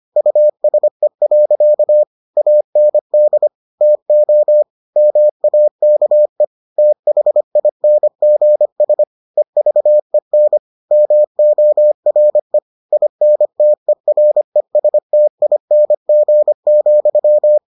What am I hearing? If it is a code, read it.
Morse code, heard as a keyed tone, about 25 wpm, 600 Hz. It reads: USE. AND TO MAKE THINGS EVEN MORE INTERESTING,